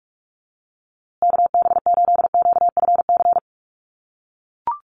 Morse code, heard as a keyed tone, about 44 wpm, 700 Hz. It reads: KB8QFC
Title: USA Call Signs Repeated 2x 45wpm